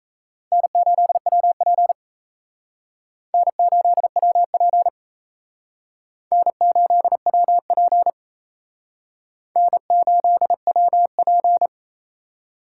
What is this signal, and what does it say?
Morse code, heard as a keyed tone, about 38 wpm, 700 Hz. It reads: N8WP N8WP N8WP N8WP